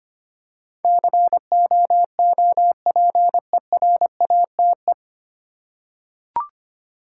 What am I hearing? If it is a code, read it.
Morse code, heard as a keyed tone, about 25 wpm, 690 Hz. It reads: COOPERATE
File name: practice / 25wpm / 9 Letter Words - First 3 Letters Given - 25wpm